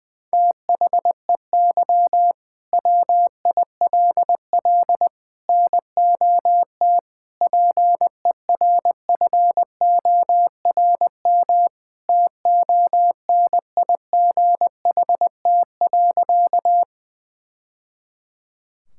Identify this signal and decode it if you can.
Morse code, heard as a keyed tone, about 21 wpm, 700 Hz. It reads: THEY WILL NOT PERFORM TONIGHT.
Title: Sentences from Top 1000 Words - Review Part 5 - 20wpm